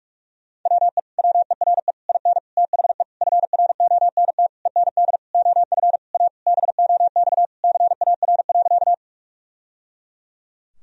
Morse code, heard as a keyed tone, about 46 wpm, 700 Hz. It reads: WE WERE IN THE FRONT END OF A BOX CAR.
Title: Sentences from Top 1000 Words - Review Part 2 - 45wpm